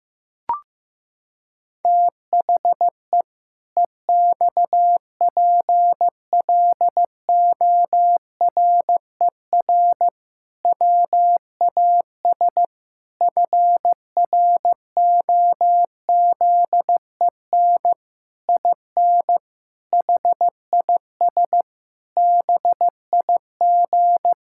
Morse code, heard as a keyed tone, about 15 wpm, 700 Hz. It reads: THE EXPLORER WAS FROZEN IN HIS BIG
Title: Pangram Challenge v2 - 15wpm